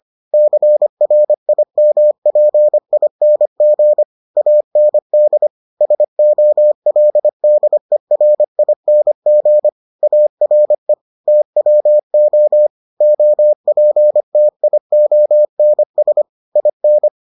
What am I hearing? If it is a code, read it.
Morse code, heard as a keyed tone, about 25 wpm, 600 Hz. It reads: CRIMPING AND SOLDERING ARE TWO OPTIONS IN